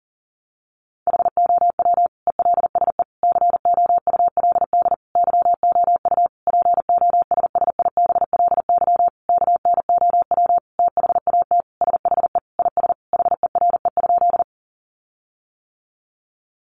Morse code, heard as a keyed tone, about 40 wpm, 700 Hz. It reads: HOW ELSE COULD YOU POSSIBLY KNOW THAT SHE IS HERE?
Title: Sentences from Top 1300 Words 40wpm